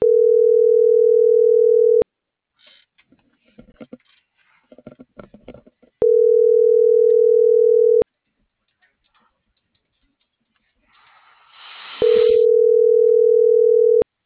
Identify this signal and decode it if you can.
Ringback tone (North America), 440 + 480 Hz.
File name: CENTRAL MABE / out antonio.garcia - .90304